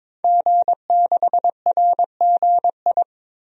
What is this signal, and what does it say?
Morse code, heard as a keyed tone, about 22 wpm, 700 Hz. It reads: G6RGI
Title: DXCC-100 Call Signs - 5 Characters - Repeated 2x - 22wpm